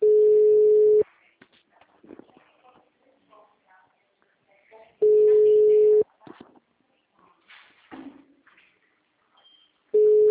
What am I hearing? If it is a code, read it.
Ringback tone (Europe, 425 Hz), with speech.